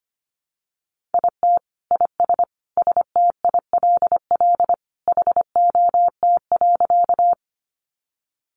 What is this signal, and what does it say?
Morse code, heard as a keyed tone, about 25 wpm, 700 Hz. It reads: IT IS STILL HOT.
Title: Additional Easy Phrases 25wpm